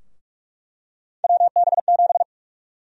Morse code, heard as a keyed tone, about 44 wpm, 700 Hz. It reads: WB7
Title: USA Call Sign Prefixes - Repeated 2x - 45wpm